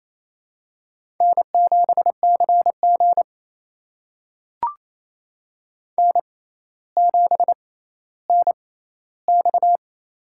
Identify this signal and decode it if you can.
Morse code, heard as a keyed tone, about 28 wpm, spaced as for 14 wpm, 700 Hz. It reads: N7CG N7NX
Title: USA Call Signs - 6x Character Spacing - 28wpm